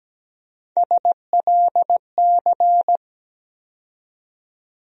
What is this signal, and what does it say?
Morse code, heard as a keyed tone, about 17 wpm, 700 Hz. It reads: SLC
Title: USA Call Sign Postfixes - 3 Letters 17wpm